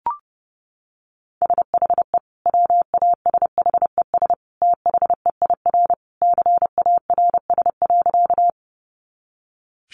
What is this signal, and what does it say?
Morse code, heard as a keyed tone, about 30 wpm, 700 Hz. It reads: SHE WASHES THEIR CARS.
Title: Easy Phrases 30wpm